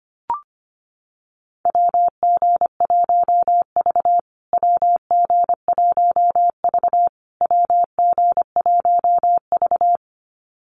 Morse code, heard as a keyed tone, about 25 wpm, 700 Hz. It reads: WG1V WG1V WG1V
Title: USA Call Signs Repeated 3x 25wpm